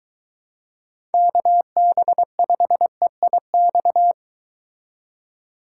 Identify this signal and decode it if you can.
Morse code, heard as a keyed tone, about 22 wpm, 700 Hz. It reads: KB5EIX